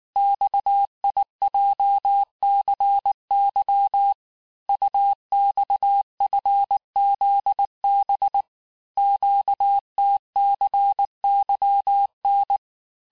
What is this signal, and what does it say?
Morse code, heard as a keyed tone, about 19 wpm, 800 Hz. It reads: XIJCY UXFZB QTCYN
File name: cours-du-mercredi / audios / cours-869-texte-CHIFFRE-vitesse-18-MPM